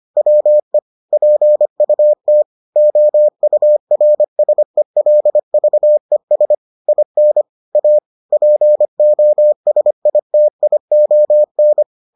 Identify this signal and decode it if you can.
Morse code, heard as a keyed tone, about 25 wpm, 600 Hz. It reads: WE PUT OURSELVES IN A POSITION